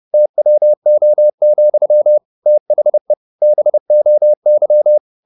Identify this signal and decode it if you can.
Morse code, heard as a keyed tone, about 30 wpm, 600 Hz. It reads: TWO, THE BOY
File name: the old man and the sea / the old man and the sea 1 30wpm